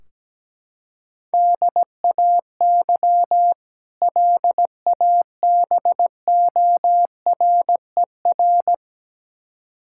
Morse code, heard as a keyed tone, about 17 wpm, 700 Hz. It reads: DAY LABORER
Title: K1USN SST QSO Practice 17wpm